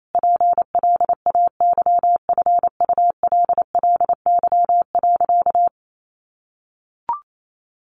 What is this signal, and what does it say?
Morse code, heard as a keyed tone, about 28 wpm, 700 Hz. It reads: PLAYFULLY.
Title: A Sound Of Thunder 28wpm